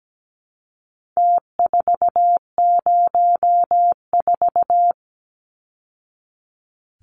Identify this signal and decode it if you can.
Morse code, heard as a keyed tone, about 17 wpm, 700 Hz. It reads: T404
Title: Numbers - Contest Style - 4-Digit 17wpm